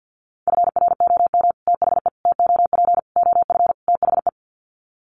Morse code, heard as a keyed tone, about 50 wpm, 700 Hz. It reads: FROM THE TOP OF THE
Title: Sets of 5 Words Encore 50wpm